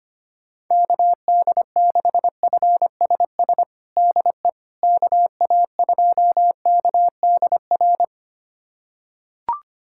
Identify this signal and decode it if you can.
Morse code, heard as a keyed tone, about 25 wpm, 700 Hz. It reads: KD6FSS DE KA2KDR
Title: USA Callsign DE Callsign 25wpm